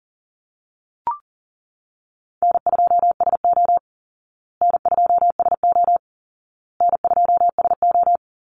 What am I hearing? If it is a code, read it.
Morse code, heard as a keyed tone, about 40 wpm, 700 Hz. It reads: N2SO N2SO N2SO
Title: Call Signs Repeated 2x - 40wpm